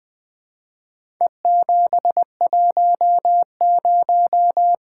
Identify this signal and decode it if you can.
Morse code, heard as a keyed tone, about 20 wpm, 700 Hz. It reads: E710
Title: Four Letter-Number Combinations - 20wpm